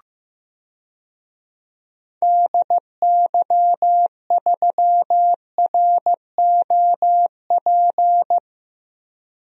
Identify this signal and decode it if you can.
Morse code, heard as a keyed tone, about 15 wpm, 700 Hz. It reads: DY3ROP
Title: DXCC-100 Call Signs - 6 Characters - Repeated 2x - 15wpm